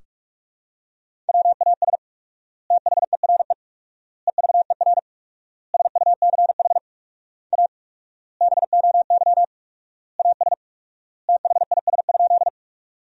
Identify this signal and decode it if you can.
Morse code, heard as a keyed tone, about 46 wpm, 700 Hz. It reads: WAS THERE EVER SUCH A BOY AS THIS?
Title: 4x Word-Spacing - Sentences from Top 400 Words - 45wpm